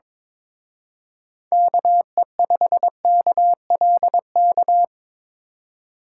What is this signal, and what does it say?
Morse code, heard as a keyed tone, about 26 wpm, 700 Hz. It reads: KE5KLK
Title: Speed Racing - USA 2x3 Call Signs 22wpm